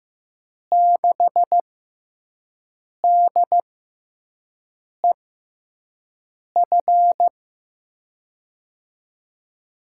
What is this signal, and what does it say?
Morse code, heard as a keyed tone, about 15 wpm, spaced as for 5 wpm, 700 Hz. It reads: 6DEF
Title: USA Call Signs - 6 Characters - 6x Character Spacing - 15wpm